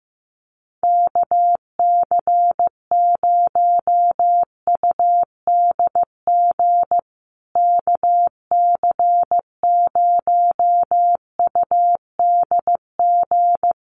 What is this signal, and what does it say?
Morse code, heard as a keyed tone, about 15 wpm, 700 Hz. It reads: KC0UDG KC0UDG
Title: USA Call Signs Repeated 3x 15wpm